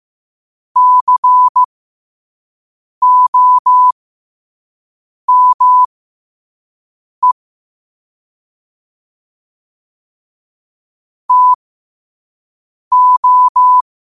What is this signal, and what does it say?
Morse code, heard as a keyed tone, about 15 wpm, spaced as for 5 wpm, 1000 Hz. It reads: COME TO
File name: 2026-02-15-5wpm